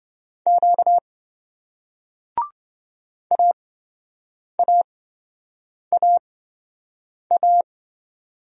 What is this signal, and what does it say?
Morse code, heard as a keyed tone, about 28 wpm, spaced as for 7 wpm, 700 Hz. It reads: Q AAAA